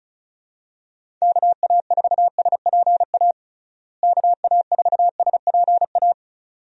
Morse code, heard as a keyed tone, about 36 wpm, 700 Hz. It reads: KA4SPA KA4SPA
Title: USA Call Signs - 6 Characters - Repeated 2x - 35wpm